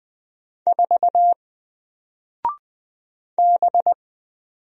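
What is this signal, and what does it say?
Morse code, heard as a keyed tone, about 20 wpm, 700 Hz. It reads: 4 B
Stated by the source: Single letter and number Morse drill at 20 wpm, from N4LCD.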